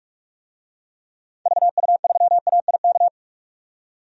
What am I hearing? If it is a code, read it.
Morse code, heard as a keyed tone, about 45 wpm, 700 Hz. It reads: VU3AIK
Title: DXCC-100 Call Signs - 6 Characters - 45wpm